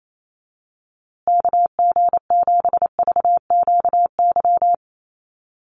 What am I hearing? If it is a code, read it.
Morse code, heard as a keyed tone, about 28 wpm, 700 Hz. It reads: KG7VQY
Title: USA Call Signs Repeated 3x 28wpm